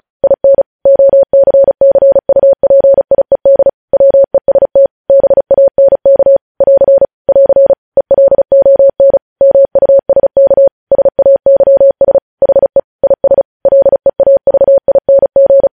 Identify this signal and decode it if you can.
Morse code, heard as a keyed tone, about 36 wpm, 550 Hz. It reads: IN OCCUPIED WEST BANK <AR> <AR> ELON MUSK SAYS HE IS LEAVING
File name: podcast / News headlines-1x-35x35